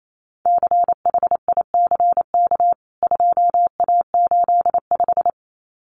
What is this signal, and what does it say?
Morse code, heard as a keyed tone, about 28 wpm, 700 Hz. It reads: CHICK 2A85
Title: CWT Exchange 28wpm